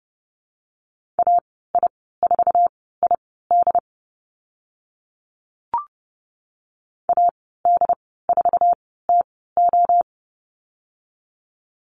Morse code, heard as a keyed tone, about 30 wpm, spaced as for 17 wpm, 700 Hz. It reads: AI4ID AD4TO
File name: Lesson 15g - Number 5 - Call Signs 30wpm